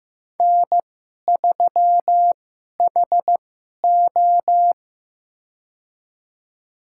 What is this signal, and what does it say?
Morse code, heard as a keyed tone, about 15 wpm, spaced as for 11 wpm, 700 Hz. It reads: N3HO
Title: USA Call Signs - 4 Characters - 2x Character Spacing - 15wpm